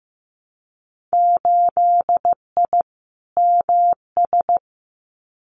Morse code, heard as a keyed tone, about 15 wpm, 700 Hz. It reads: 8I MS
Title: Winter Field Day Exchanges Encore 15wpm